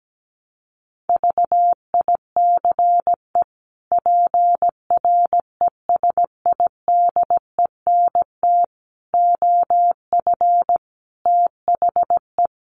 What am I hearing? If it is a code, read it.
Morse code, heard as a keyed tone, about 17 wpm, 700 Hz. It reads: VICE PRESIDENT OF THE
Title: Sets of 5 Words Encore2 17wpm